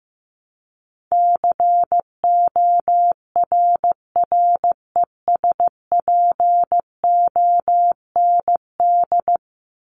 Morse code, heard as a keyed tone, about 15 wpm, 700 Hz. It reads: CORRESPOND
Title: ING Suffix - Top 100 Words with Base Words 15wpm